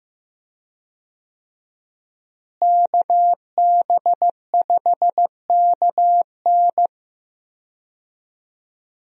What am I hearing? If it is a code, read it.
Morse code, heard as a keyed tone, about 15 wpm, 700 Hz. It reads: KB5KN
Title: USA Call Signs - 5 Characters - Repeated 2x - 15wpm